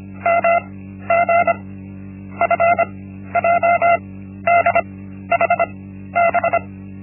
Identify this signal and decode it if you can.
Morse code, heard as a keyed tone, about 25 wpm, 680 Hz. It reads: M G F J D H B
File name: Linksmakalnis-1